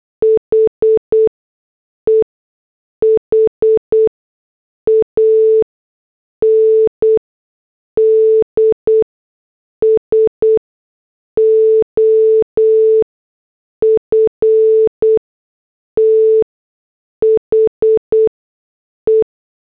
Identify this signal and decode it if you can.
Morse code, heard as a keyed tone, about 8 wpm, spaced as for 6 wpm, 440 Hz. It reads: HE HANDS OF THE